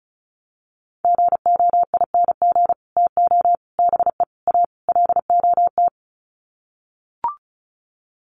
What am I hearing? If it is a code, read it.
Morse code, heard as a keyed tone, about 34 wpm, 700 Hz. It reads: GOING TO BE A LOT
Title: Sets of 5 Words Encore 35wpm